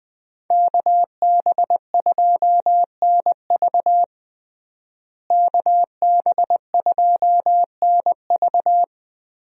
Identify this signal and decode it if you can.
Morse code, heard as a keyed tone, about 20 wpm, 700 Hz. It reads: KB2NV KB2NV
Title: USA Call Signs - 5 Characters - Repeated 2x - 20wpm